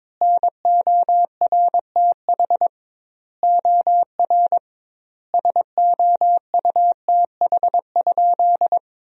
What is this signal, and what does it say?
Morse code, heard as a keyed tone, about 22 wpm, 700 Hz. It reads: NORTH OR SOUTH?
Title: 2x Word-Spacing - Sentences from Top 400 Words - 22wpm